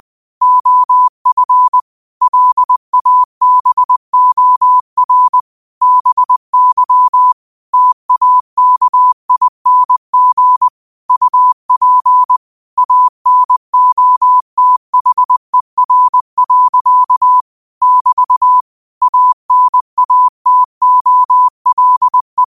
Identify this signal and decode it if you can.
Morse code, heard as a keyed tone, about 20 wpm, 1000 Hz. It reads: OF LABOR BY TAKING UP ANOTHER. <BT> ANATOLE
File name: qotdfiles / 20wpm